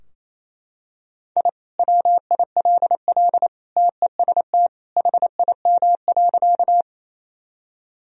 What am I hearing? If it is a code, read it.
Morse code, heard as a keyed tone, about 28 wpm, 700 Hz. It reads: I WILL TEST HIM.
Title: Sentences from Top 1000 Words - Review Part 3 - 28wpm